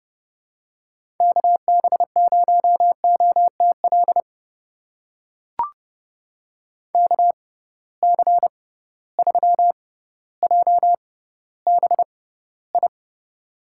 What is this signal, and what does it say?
Morse code, heard as a keyed tone, about 32 wpm, spaced as for 16 wpm, 700 Hz. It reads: KB0OTL KC3JBI
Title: USA Call Signs - 6 Characters - 6x Character Spacing - 30wpm